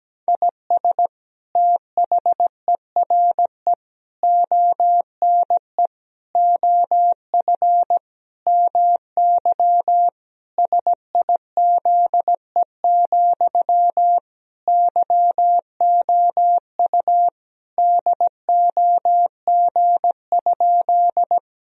Morse code, heard as a keyed tone, about 17 wpm, 700 Hz. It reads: IS THERE ONE OF MY SIZE, YOU DOG?
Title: Sentences from Top 1000 Words - Review Part 2 - 17wpm